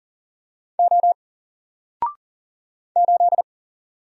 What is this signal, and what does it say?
Morse code, heard as a keyed tone, about 40 wpm, 700 Hz. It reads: O 8
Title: Single Character (Letter:Number:Punctuation:Prosign) 40wpm